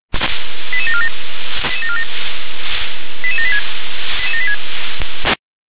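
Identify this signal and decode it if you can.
Five-tone selective call (ZVEI), heard as quick runs of five tones, tones of about 80 ms. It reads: 88747 88786
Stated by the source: ZVEI five-tone selective call received on an SDR.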